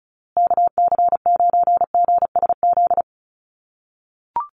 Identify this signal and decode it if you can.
Morse code, heard as a keyed tone, about 34 wpm, 700 Hz. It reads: KC9GSZ
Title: USA Call Signs Repeated 3x 35wpm